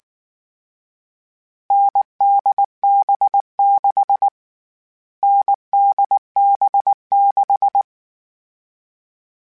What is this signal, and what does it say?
Morse code, heard as a keyed tone, about 19 wpm, 800 Hz. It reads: NDB6 NDB6
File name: Short Runs / Short Runs-mp3s / NDB NDB6 @18